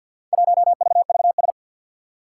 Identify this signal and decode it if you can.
Morse code, heard as a keyed tone, about 53 wpm, 700 Hz. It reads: 1VVS
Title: Four Challenging Letter-Number Combinations 50wpm